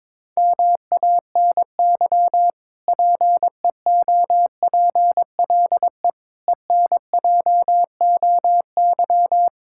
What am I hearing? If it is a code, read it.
Morse code, heard as a keyed tone, about 21 wpm, 700 Hz. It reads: MANY PEOPLE ENJOY
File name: Difficult Sentences with Hints - 22wpm